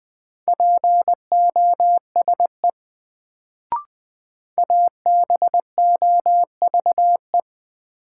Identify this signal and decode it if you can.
Morse code, heard as a keyed tone, about 20 wpm, 700 Hz. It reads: POSE ABOVE
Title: Sentences from Top 1000 Words - Review Part 2 - New Word Drill - 20wpm